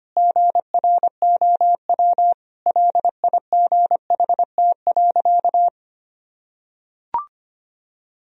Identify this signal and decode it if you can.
Morse code, heard as a keyed tone, about 25 wpm, 700 Hz. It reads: GROW LIGHT.
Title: Sentences from Top 1000 Words - Review Part 1 - 25wpm